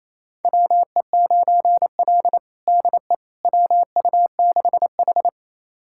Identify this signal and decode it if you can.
Morse code, heard as a keyed tone, about 28 wpm, 700 Hz. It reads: WE9L DE WU6H
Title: USA Callsign DE Callsign - 4 Characters - 28wpm